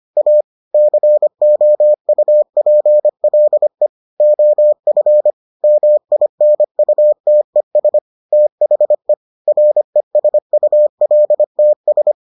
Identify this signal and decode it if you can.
Morse code, heard as a keyed tone, about 25 wpm, 600 Hz. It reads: A COUPLE OF MINUTES THE RESULTS